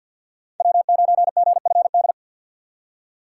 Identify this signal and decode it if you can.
Morse code, heard as a keyed tone, about 52 wpm, 700 Hz. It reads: W9GFD
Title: USA Call Signs - 5 Characters - 4x Character Spacing - 50wpm